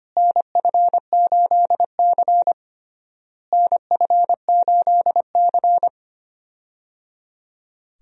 Morse code, heard as a keyed tone, about 25 wpm, 700 Hz. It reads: NF8C NF8C
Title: USA Call Signs - 4 Characters - Repeated 3x - 25wpm